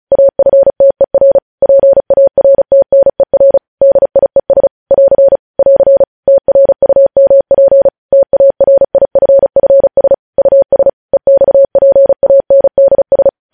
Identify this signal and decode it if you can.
Morse code, heard as a keyed tone, about 36 wpm, 550 Hz. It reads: AFTER PARTNER DIES <AR> <AR> TRUMP TARIFFS US EXPANDS
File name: podcast / News headlines-1x-35x35